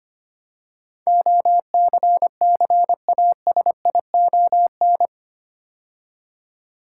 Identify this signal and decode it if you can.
Morse code, heard as a keyed tone, about 25 wpm, 700 Hz. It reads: OCCASION
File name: 8 Letter Words - First 3 Letters Given - 25wpm